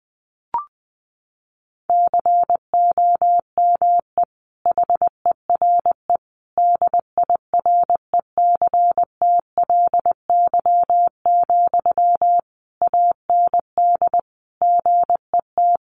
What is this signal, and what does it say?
Morse code, heard as a keyed tone, about 20 wpm, 700 Hz. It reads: COME HERE DIRECTLY, AND GET
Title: Sentences from Top 800 Words 20wpm